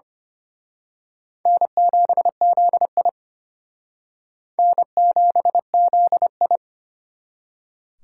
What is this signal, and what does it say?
Morse code, heard as a keyed tone, about 31 wpm, 700 Hz. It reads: N7ZI N7ZI
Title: Speed Racing - USA 1x2 Call Signs 25wpm